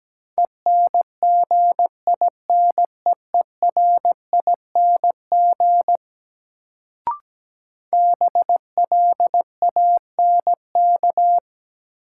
Morse code, heard as a keyed tone, about 17 wpm, 700 Hz. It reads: ENGINEERING BLANK
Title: Sentences from Top 1500 Words - New Word Drill - 17wpm